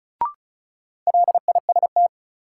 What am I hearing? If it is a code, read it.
Morse code, heard as a keyed tone, about 36 wpm, 700 Hz. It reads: LIST
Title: Sentences from Top 400 Words - New Word Drill - 35wpm